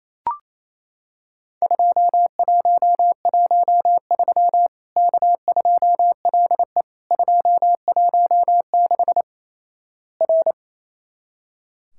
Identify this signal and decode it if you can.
Morse code, heard as a keyed tone, about 28 wpm, 700 Hz. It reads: 2113 K2LE 216 R
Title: Worked All Europe DX Contest - QTC Exchange - 28wpm